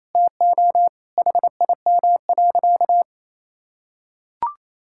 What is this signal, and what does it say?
Morse code, heard as a keyed tone, about 27 wpm, 700 Hz. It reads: TO HIM.
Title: Sentences from Top 1000 Words - Review Part 5 - 28wpm